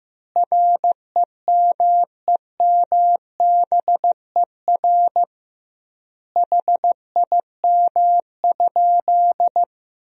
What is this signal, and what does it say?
Morse code, heard as a keyed tone, about 15 wpm, 700 Hz. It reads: REMEMBER HIM?
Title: 2x Word-Spacing - Sentences from Top 400 Words - 15wpm